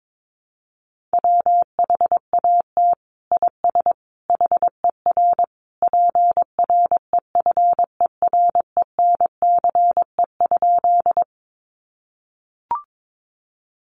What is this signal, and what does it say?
Morse code, heard as a keyed tone, about 22 wpm, 700 Hz. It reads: WHAT IS HER PREFERENCE?
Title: Sentences from Top 1500 Words 22wpm